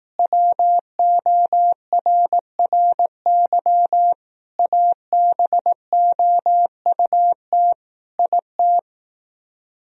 Morse code, heard as a keyed tone, about 18 wpm, 700 Hz. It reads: WORRY ABOUT IT
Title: Speed-Racing - Sets of 5 Words 15wpm